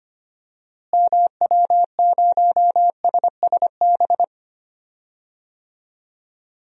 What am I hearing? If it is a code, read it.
Morse code, heard as a keyed tone, about 25 wpm, 700 Hz. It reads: MW0SSB
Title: DXCC-100 Call Signs - 6 Characters - Repeated 3x - 25wpm